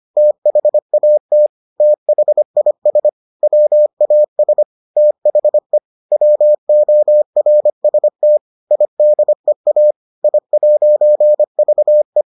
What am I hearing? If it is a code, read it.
Morse code, heard as a keyed tone, about 25 wpm, 600 Hz. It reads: THAT THIS WAS THE WORST IDEA I'VE